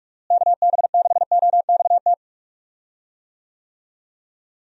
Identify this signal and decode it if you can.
Morse code, heard as a keyed tone, about 46 wpm, 700 Hz. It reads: KB6OXT
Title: USA Call Signs - 6 Characters - Repeated 2x - 45wpm